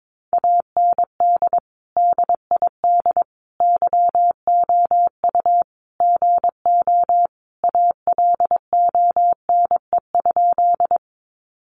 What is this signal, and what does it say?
Morse code, heard as a keyed tone, about 22 wpm, 700 Hz. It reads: AND DID YOU GO ALONE?
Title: Sentences from Top 1000 Words 22wpm